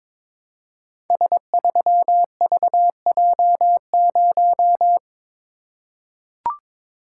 Morse code, heard as a keyed tone, about 22 wpm, 700 Hz. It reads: S3VJ0
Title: Code Groups - 5 Letter:Numbers - 22wpm